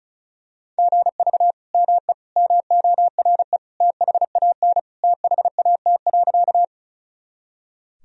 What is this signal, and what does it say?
Morse code, heard as a keyed tone, about 34 wpm, 690 Hz. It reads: GV ME MORE THAN THAT.
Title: Sentences from Top 200 Words and Abbreviations 35wpm